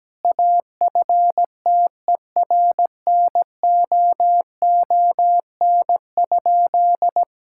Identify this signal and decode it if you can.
Morse code, heard as a keyed tone, about 17 wpm, 700 Hz. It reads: AFTERNOON?
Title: Sentences from Top 1600 Words 17wpm